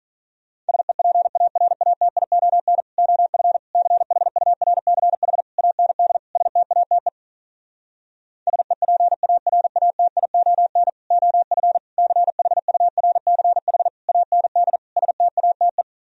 Morse code, heard as a keyed tone, about 46 wpm, 700 Hz. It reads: SEPARATION OF CHURCH AND STATE SEPARATION OF CHURCH AND STATE